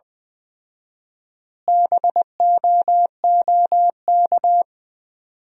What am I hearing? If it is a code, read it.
Morse code, heard as a keyed tone, about 20 wpm, 700 Hz. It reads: BOOK
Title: 4x Word-Spacing - Sentences from Top 400 Words - 20wpm